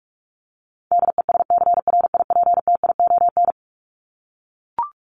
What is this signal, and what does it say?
Morse code, heard as a keyed tone, about 44 wpm, 700 Hz. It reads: DESCRIPTION
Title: Top 1001-1250 Words 45wpm